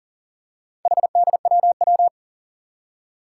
Morse code, heard as a keyed tone, about 40 wpm, 700 Hz. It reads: HDWW
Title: Four Challenging Letter-Number Combinations 40wpm